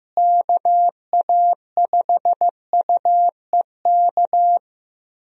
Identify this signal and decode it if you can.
Morse code, heard as a keyed tone, about 15 wpm, 700 Hz. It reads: KA5UEK
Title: USA Call Signs - 6 Characters - Repeated 3x - 15wpm